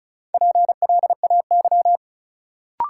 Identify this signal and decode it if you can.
Morse code, heard as a keyed tone, about 36 wpm, 700 Hz. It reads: PLAY
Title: Sentences from Top 200 Words - New Word Drill - 35wpm